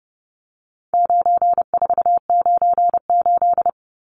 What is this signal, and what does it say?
Morse code, heard as a keyed tone, about 30 wpm, 700 Hz. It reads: 9498
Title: Numbers - 4-Digit 30wpm